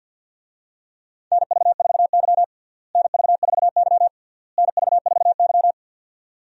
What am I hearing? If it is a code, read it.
Morse code, heard as a keyed tone, about 52 wpm, 700 Hz. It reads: NV4Y NV4Y NV4Y
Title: USA Call Signs - 4 Characters - Repeated 3x - 50wpm